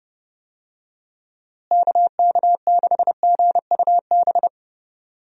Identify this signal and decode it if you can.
Morse code, heard as a keyed tone, about 30 wpm, 700 Hz. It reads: KK6GUB
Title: USA Call Signs - 6 Characters - Repeated 3x - 30wpm